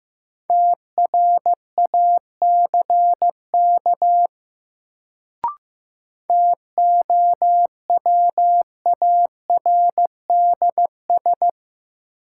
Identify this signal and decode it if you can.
Morse code, heard as a keyed tone, about 15 wpm, 700 Hz. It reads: TRACK TOWARDS
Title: Sentences from Top 1100 Words - New Word Drill Alt - 15wpm